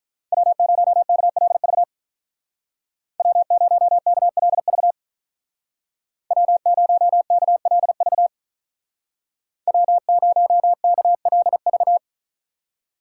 Morse code, heard as a keyed tone, about 45 wpm, 700 Hz. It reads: W0KLV W0KLV W0KLV W0KLV